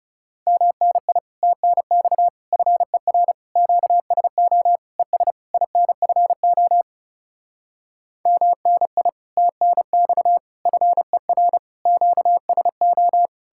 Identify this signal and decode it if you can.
Morse code, heard as a keyed tone, about 40 wpm, 700 Hz. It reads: MNI TNX FER QSO ES INFO MNI TNX FER QSO